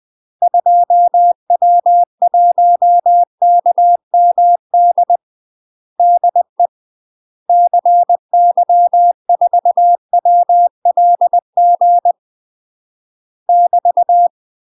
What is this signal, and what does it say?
Morse code, heard as a keyed tone, about 20 wpm, 700 Hz. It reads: 2W1KMD DE CY4WLG <BT>
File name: UK MASTER QSOs TRAINING/20WPM